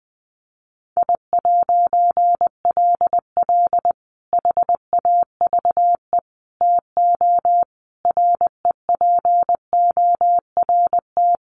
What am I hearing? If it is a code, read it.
Morse code, heard as a keyed tone, about 20 wpm, 700 Hz. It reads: I'LL HAVE TO REPORT
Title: A Sound Of Thunder 20wpm